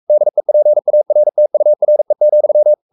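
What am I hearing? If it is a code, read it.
Morse code, heard as a keyed tone, about 44 wpm, 600 Hz. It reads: DEPARTURE,
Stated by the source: Morse code at 43 wpm sending Ken Follett's novel Never in English, from CW med Gnister.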